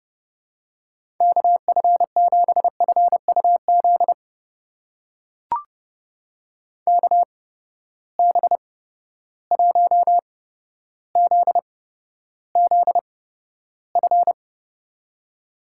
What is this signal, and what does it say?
Morse code, heard as a keyed tone, about 30 wpm, spaced as for 13 wpm, 700 Hz. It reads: KF7FUZ KB1ZZF
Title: USA Call Signs - 6 Characters - 8x Character Spacing - 30wpm